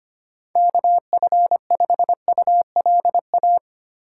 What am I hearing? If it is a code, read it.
Morse code, heard as a keyed tone, about 25 wpm, 700 Hz. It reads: KF5ULA
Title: USA Call Signs - 4x Character Spacing - 25wpm